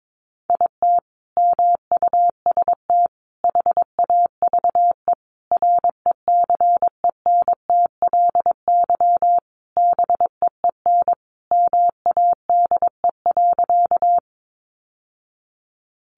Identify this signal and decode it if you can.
Morse code, heard as a keyed tone, about 22 wpm, 700 Hz. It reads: IT MUST HAVE RECENTLY BEEN MADE.
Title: Sentences from Top 1200 Words 22wpm